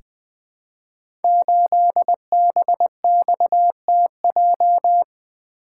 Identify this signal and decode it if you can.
Morse code, heard as a keyed tone, about 20 wpm, 700 Hz. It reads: 8BXTJ
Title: Code Groups - 5 Letter:Numbers - 20wpm